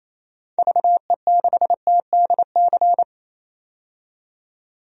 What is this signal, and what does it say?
Morse code, heard as a keyed tone, about 28 wpm, 700 Hz. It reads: VE6TDC
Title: DXCC-100 Call Signs - 6 Characters - 28wpm